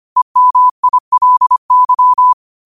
Morse code, heard as a keyed tone, about 25 wpm, 1000 Hz. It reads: EMILY